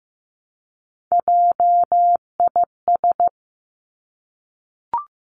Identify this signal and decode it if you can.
Morse code, heard as a keyed tone, about 15 wpm, 700 Hz. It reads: JIS
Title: Three Challenging Letter-Number Combinations 15wpm